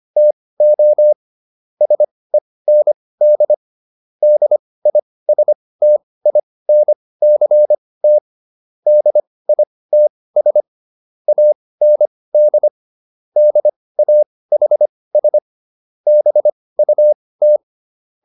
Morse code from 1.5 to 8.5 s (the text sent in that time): SEND DISTINCT